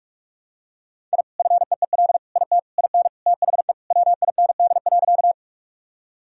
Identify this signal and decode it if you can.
Morse code, heard as a keyed tone, about 44 wpm, 700 Hz. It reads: I FEEL IT IN THE WIND.
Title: Sentences from Top 1000 Words - Review Part 2 - 45wpm